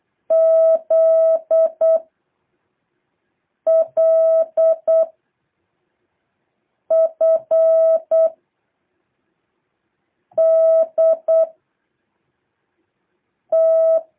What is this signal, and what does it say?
Morse code, heard as a keyed tone, about 8 wpm, 650 Hz. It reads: Z L F D T
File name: jh8cbh / 2023cw